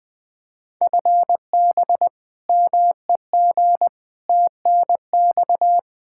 Morse code, heard as a keyed tone, about 20 wpm, 700 Hz. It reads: FB MEG TNX
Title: Practice QSO - Exchange 3 - More Info - 20wpm